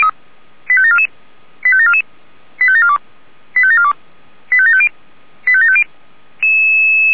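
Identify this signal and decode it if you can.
Five-tone selective call (ZVEI), heard as quick runs of five tones, tones of about 70 ms. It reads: 85744 85744 85742 85742 85750 85750
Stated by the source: ZVEI five-tone selective call, address 85749, received on an SDR by Feuerwehr Orlamünde.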